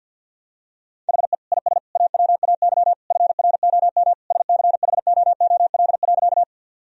Morse code, heard as a keyed tone, about 46 wpm, 700 Hz. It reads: HE IS AWAY FROM SCHOOL.
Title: Sentences from Top 1000 Words - Review Part 1 - 50wpm